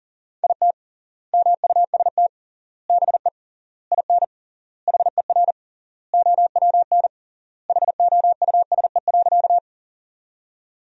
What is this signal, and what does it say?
Morse code, heard as a keyed tone, about 37 wpm, 700 Hz. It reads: IT MUST BE IN HER OWN HOUSE.
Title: 3x Word-Spacing - Sentences from Top 300 Words - 40wpm